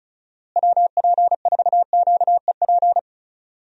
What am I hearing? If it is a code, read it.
Morse code, heard as a keyed tone, about 36 wpm, 700 Hz. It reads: WP4QEP
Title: USA Call Signs - 6 Characters - Repeated 3x - 35wpm